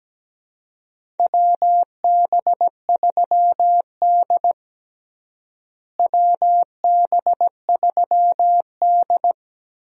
Morse code, heard as a keyed tone, about 17 wpm, 700 Hz. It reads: WB3D WB3D
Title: USA Call Signs - 4 Characters - Repeated 3x - 17wpm